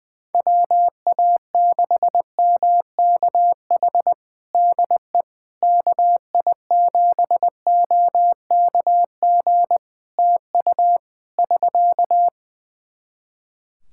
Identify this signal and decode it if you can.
Morse code, heard as a keyed tone, about 20 wpm, 700 Hz. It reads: WA6MKH DE KI7OKG TU <SK>
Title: Practice QSO - Exchange 4 - Ending - 20wpm